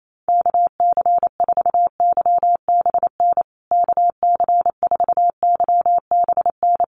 Morse code, heard as a keyed tone, about 28 wpm, 700 Hz. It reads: KC4YBN KC4YBN
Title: USA Call Signs Repeated 3x 28wpm